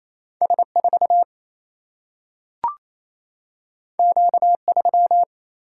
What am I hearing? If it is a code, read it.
Morse code, heard as a keyed tone, about 28 wpm, 700 Hz. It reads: S4 Q3
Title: Single-Letter Single-Number 28wpm